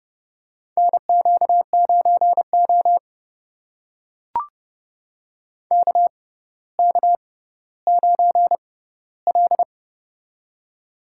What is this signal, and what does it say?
Morse code, heard as a keyed tone, about 30 wpm, spaced as for 15 wpm, 700 Hz. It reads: NQ9O KK9L
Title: USA Call Signs - 4 Characters - 6x Character Spacing - 30wpm